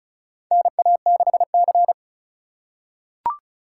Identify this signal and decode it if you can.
Morse code, heard as a keyed tone, about 36 wpm, 700 Hz. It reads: NA6C
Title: USA Call Signs - 4 Characters - 3x Character Spacing - 35wpm